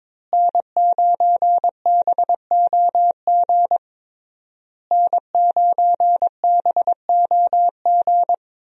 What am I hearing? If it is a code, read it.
Morse code, heard as a keyed tone, about 22 wpm, 700 Hz. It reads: N9BOG N9BOG
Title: USA Call Signs - 5 Characters - Repeated 3x - 22wpm